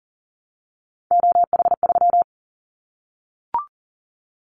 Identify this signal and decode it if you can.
Morse code, heard as a keyed tone, about 40 wpm, 700 Hz. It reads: OH3
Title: Callsign Prefixes for Top 25 DXCC - 40wpm